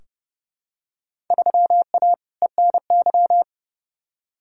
Morse code, heard as a keyed tone, about 30 wpm, 700 Hz. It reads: 3A ENY
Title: Field Day Exchanges - 30wpm